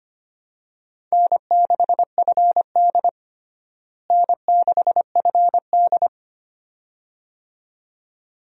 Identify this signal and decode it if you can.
Morse code, heard as a keyed tone, about 25 wpm, 700 Hz. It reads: N6FD N6FD
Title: USA Call Signs - 4 Characters - Repeated 2x - 25wpm